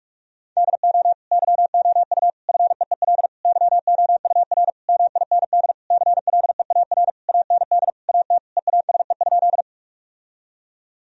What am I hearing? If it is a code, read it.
Morse code, heard as a keyed tone, about 46 wpm, 700 Hz. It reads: DO YOU FEEL YOUR MIND CLEAR AND AT EASE?
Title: Sentences from Top 1000 Words - Review Part 1 - 45wpm